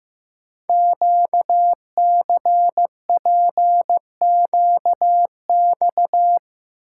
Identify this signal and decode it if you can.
Morse code, heard as a keyed tone, about 15 wpm, 700 Hz. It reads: QCPQX
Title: Code Groups - 5 Letters - 15wpm